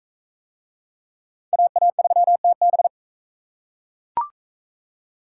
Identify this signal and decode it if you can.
Morse code, heard as a keyed tone, about 37 wpm, 700 Hz. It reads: AA3TB